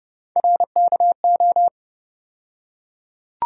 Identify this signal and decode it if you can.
Morse code, heard as a keyed tone, about 30 wpm, 700 Hz. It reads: RKO
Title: USA Call Sign Postfixes - 3 Letters 30wpm